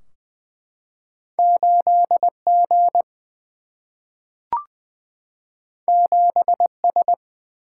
Morse code, heard as a keyed tone, about 20 wpm, 700 Hz. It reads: 8 G 7 S
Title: Single-Letter Single-Number 20wpm